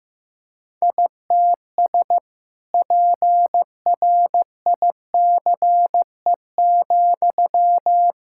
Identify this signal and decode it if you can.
Morse code, heard as a keyed tone, about 15 wpm, 700 Hz. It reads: ITS PRICE,
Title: Sentences from Top 1000 Words - Review Part 5 - 15wpm